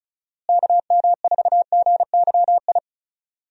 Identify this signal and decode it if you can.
Morse code, heard as a keyed tone, about 34 wpm, 700 Hz. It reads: KM4GYI
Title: USA Call Signs - 6 Characters - Repeated 2x - 35wpm